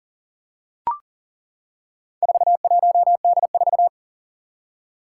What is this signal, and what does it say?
Morse code, heard as a keyed tone, about 40 wpm, 700 Hz. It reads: 41D4
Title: Four Challenging Letter-Number Combinations 40wpm